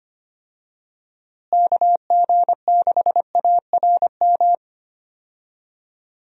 Morse code, heard as a keyed tone, about 25 wpm, 700 Hz. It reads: KG6ARM
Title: USA Call Signs - 6 Characters - Repeated 2x - 25wpm